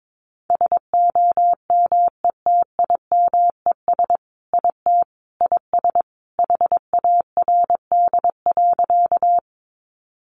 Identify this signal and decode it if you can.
Morse code, heard as a keyed tone, about 22 wpm, 700 Hz. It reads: SOMETIMES IT IS HARD.
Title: Sentences from Top 600 Words 22wpm